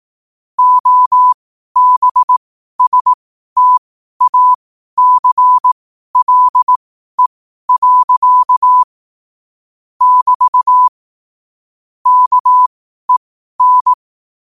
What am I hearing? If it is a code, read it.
Morse code, heard as a keyed tone, about 18 wpm, spaced as for 12 wpm, 1000 Hz. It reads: OBSTACLE. <BT> KEN
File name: qotdfiles / 12wpm